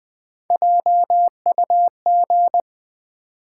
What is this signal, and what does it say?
Morse code, heard as a keyed tone, about 20 wpm, 700 Hz. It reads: JUG
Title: Easy Words with Hints - 20wpm